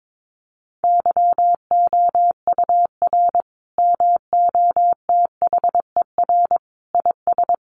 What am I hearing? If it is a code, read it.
Morse code, heard as a keyed tone, about 22 wpm, 700 Hz. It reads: YOUR MOTHER IS
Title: Sentences from Top 1100 Words 22wpm